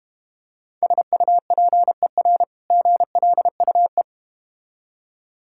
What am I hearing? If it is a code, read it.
Morse code, heard as a keyed tone, about 31 wpm, 700 Hz. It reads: SUPER GLUE
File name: practice / 25wpm / Difficult Words - Speed-Racing with Hints - 25wpm